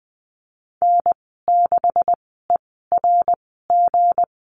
Morse code, heard as a keyed tone, about 20 wpm, spaced as for 15 wpm, 700 Hz. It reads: N6ERG
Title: 173 Lesson 28g - Number 9 - Call Signs 20wpm